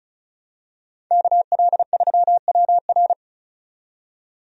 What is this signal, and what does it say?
Morse code, heard as a keyed tone, about 34 wpm, 700 Hz. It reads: KL3WR
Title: USA Call Signs - 5 Characters - 35wpm